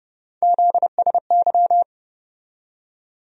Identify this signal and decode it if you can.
Morse code, heard as a keyed tone, about 30 wpm, 700 Hz. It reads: ZSY
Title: USA Call Sign Postfixes - 3 Letters 30wpm